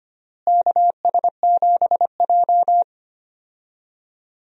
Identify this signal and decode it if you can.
Morse code, heard as a keyed tone, about 25 wpm, 700 Hz. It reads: KS7J
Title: USA Call Signs - 4 Characters - Repeated 2x - 25wpm